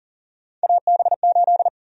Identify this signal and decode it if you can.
Morse code, heard as a keyed tone, about 40 wpm, 700 Hz. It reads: AB8
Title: USA Call Sign Prefixes - 40wpm